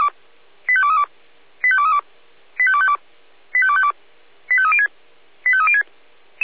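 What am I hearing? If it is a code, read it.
Five-tone selective call (ZVEI), heard as quick runs of five tones, tones of about 70 ms. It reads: 85232 85232 85252 85252 85306 85306
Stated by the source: Five-tone ZVEI selective call received by SDR.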